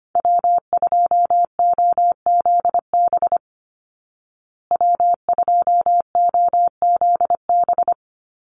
Morse code, heard as a keyed tone, about 25 wpm, 700 Hz. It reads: W2OZB W2OZB
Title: Call Signs Repeated 3x - 25wpm